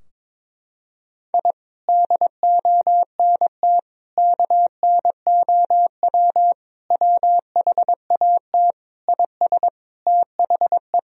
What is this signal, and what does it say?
Morse code, heard as a keyed tone, about 22 wpm, 700 Hz. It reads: I DONT KNOW WHAT IS THE
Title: Sentences from Top 1000 Words - Review Part 4 - 22wpm